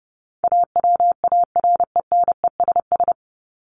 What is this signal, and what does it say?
Morse code, heard as a keyed tone, about 30 wpm, 700 Hz. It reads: AWARENESS
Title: Top 1501-1750 Words 30wpm